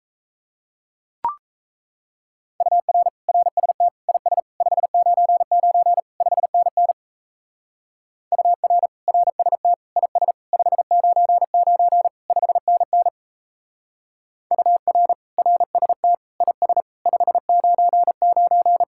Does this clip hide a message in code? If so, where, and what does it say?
Yes: Morse code from 2.6 to 18.8 s, heard as a keyed tone, about 36 wpm, 700 Hz. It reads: UR RST IS 599 5NN UR RST IS 599 5NN UR RST IS 599